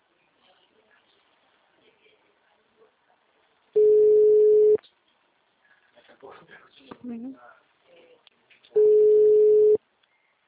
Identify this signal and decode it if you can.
Ringback tone (Europe, 425 Hz), with speech.
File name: out- -2029- - - .11432